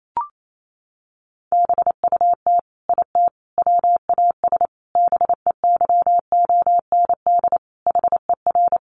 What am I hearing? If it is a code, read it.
Morse code, heard as a keyed tone, about 28 wpm, 700 Hz. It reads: BUT IT WAS BEYOND HER
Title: Sentences from Top 1200 Words 28wpm